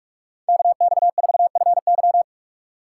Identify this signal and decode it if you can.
Morse code, heard as a keyed tone, about 46 wpm, 700 Hz. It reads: KX4FY
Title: USA Call Signs - 5 Characters - 4x Character Spacing - 45wpm